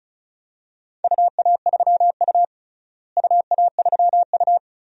Morse code, heard as a keyed tone, about 34 wpm, 700 Hz. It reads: UA3U UA3U
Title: DXCC-100 Call Signs - 4 Characters - Repeated 2x - 35wpm